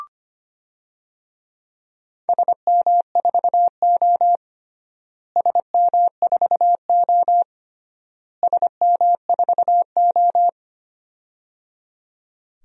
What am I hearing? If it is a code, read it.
Morse code, heard as a keyed tone, about 25 wpm, 700 Hz. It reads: SM4O SM4O SM4O
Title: DXCC-100 Call Signs - 4 Characters - Repeated 3x - 25wpm